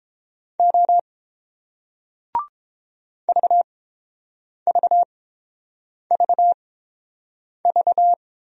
Morse code, heard as a keyed tone, about 30 wpm, spaced as for 7 wpm, 700 Hz. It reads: O VVVV